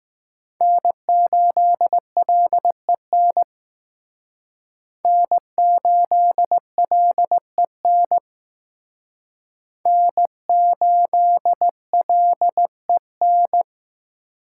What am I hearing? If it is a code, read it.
Morse code, heard as a keyed tone, about 18 wpm, 700 Hz. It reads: N8LEN N8LEN N8LEN
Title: Speed Racing - USA Call Signs 15wpm